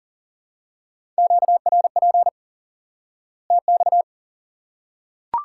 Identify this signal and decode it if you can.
Morse code, heard as a keyed tone, about 40 wpm, 700 Hz. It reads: QRP TX
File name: IQE Course 3 - Lesson 89 - YAGI - Two Elements - 40wpm